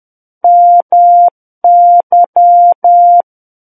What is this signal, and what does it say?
Morse code, heard as a keyed tone, about 10 wpm, 700 Hz. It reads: MY
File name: Book One / 01-Chapter 1 - 10 WPM